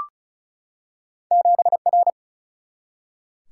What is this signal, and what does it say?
Morse code, heard as a keyed tone, about 36 wpm, 700 Hz. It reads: 7R
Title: Single-Letter Single-Number 35wpm